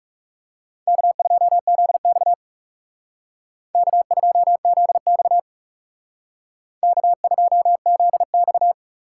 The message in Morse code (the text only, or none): K2ZX K2ZX K2ZX